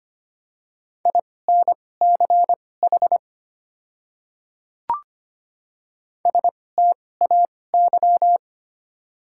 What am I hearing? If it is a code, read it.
Morse code, heard as a keyed tone, about 25 wpm, spaced as for 18 wpm, 690 Hz. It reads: INCH STAY
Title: 2x Character Spacing - Top 500 Words - 25wpm